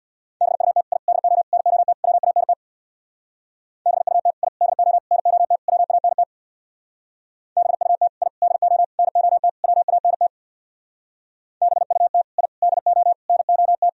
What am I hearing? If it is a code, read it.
Morse code, heard as a keyed tone, about 64 wpm, 700 Hz. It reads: BUT I DO NOT WANT BUT I DO NOT WANT BUT I DO NOT WANT BUT I DO NOT